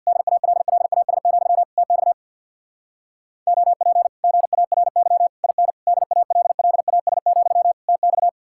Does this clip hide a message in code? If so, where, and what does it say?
Yes: Morse code from 0.1 to 8.3 s, heard as a keyed tone, about 67 wpm, 700 Hz. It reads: DALLAS, TX OP GARY IN DALLAS, TX